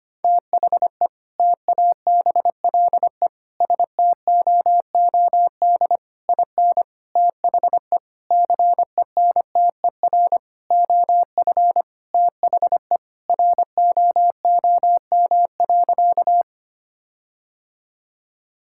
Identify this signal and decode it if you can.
Morse code, heard as a keyed tone, about 25 wpm, 700 Hz. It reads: THE TABLE STOOD IN THE CENTER OF THE ROOM.
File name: practice / 25wpm / Sentences with 4-Word Phrases - 25wpm